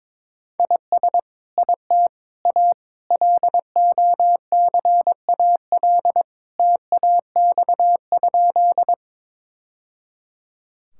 Morse code, heard as a keyed tone, about 22 wpm, 700 Hz. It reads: IS IT A LOCAL TAX?
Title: Sentences from Top 1000 Words - Review Part 5 - 22wpm